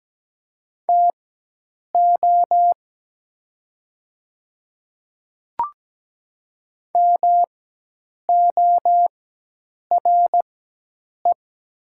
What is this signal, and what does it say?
Morse code, heard as a keyed tone, about 17 wpm, spaced as for 8 wpm, 700 Hz. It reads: TO MORE
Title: 4x Character Spacing - Top 100 Words - 17wpm